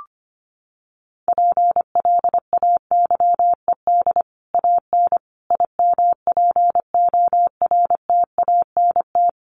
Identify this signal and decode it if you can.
Morse code, heard as a keyed tone, about 25 wpm, 700 Hz. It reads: PLAYED AN IMPORTANT
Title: Sets of 5 Words Encore2 25wpm